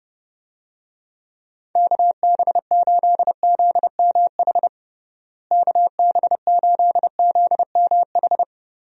Morse code, heard as a keyed tone, about 30 wpm, 700 Hz. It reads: KB8ZMH KB8ZMH
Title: USA Call Signs - 6 Characters - Repeated 3x - 30wpm